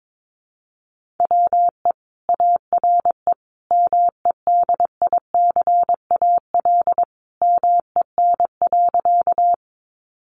Morse code, heard as a keyed tone, about 22 wpm, 700 Hz. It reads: WE ARE MEDICAL MEN.
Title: Sentences from Top 800 Words 22wpm